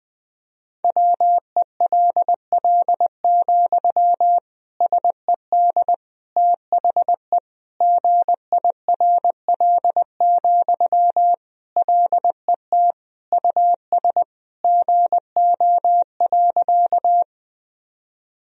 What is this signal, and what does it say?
Morse code, heard as a keyed tone, about 21 wpm, 700 Hz. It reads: WELL, SED THE GIRL, LET US GO.
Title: Sentences from Top 400 Words and Abbreviations 20wpm